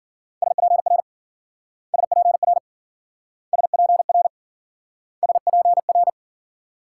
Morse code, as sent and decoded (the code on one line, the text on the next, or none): ... .--. .-. / ... .--. .-. / ... .--. .-. / ... .--. .-.
SPR SPR SPR SPR